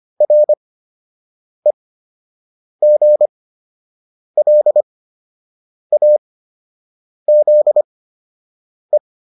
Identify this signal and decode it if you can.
Morse code, heard as a keyed tone, about 25 wpm, spaced as for 7 wpm, 600 Hz. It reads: REGLAZE